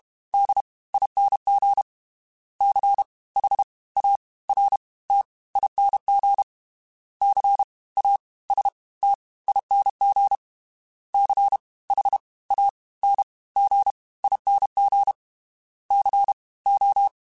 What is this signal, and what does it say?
Morse code, heard as a keyed tone, about 32 wpm, 800 Hz. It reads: D ING C H A R T ING C A S T ING C H A N G ING C O
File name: Special INGs / Special INGs-mp3s / Fast ing @30